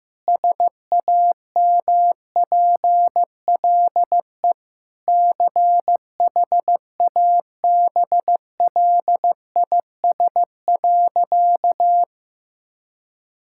Morse code, heard as a keyed tone, about 15 wpm, 700 Hz. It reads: SAMPLE CHABLIS.
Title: Pangram Challenge v2 - 15wpm